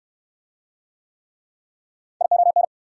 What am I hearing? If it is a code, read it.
Morse code, heard as a keyed tone, about 60 wpm, 690 Hz. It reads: ION